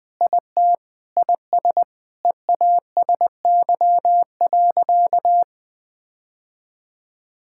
Morse code, heard as a keyed tone, about 20 wpm, 700 Hz. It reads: IT IS EASY.
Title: Sentences with 7-Word Phrases - 20wpm